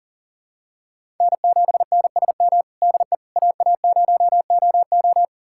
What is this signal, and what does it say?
Morse code, heard as a keyed tone, about 40 wpm, 700 Hz. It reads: N7NSM DE AA0OO
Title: USA Callsign DE Callsign - 5 Characters - 40wpm